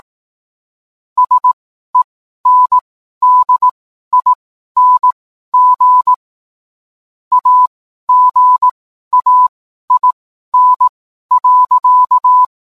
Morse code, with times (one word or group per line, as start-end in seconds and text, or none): 1.2-6.1 SENDING
7.3-12.4 AGAIN.